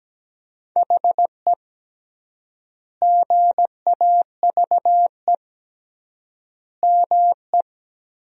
Morse code, heard as a keyed tone, about 17 wpm, 700 Hz. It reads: HE GAVE ME
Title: 3x Word-Spacing - Sentences from Top 500 Words - 17wpm